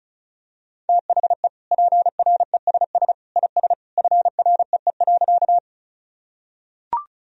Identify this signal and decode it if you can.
Morse code, heard as a keyed tone, about 34 wpm, 700 Hz. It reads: THE PRESS IS FREE.
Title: Sentences from Top 1000 Words - Review Part 1 - 35wpm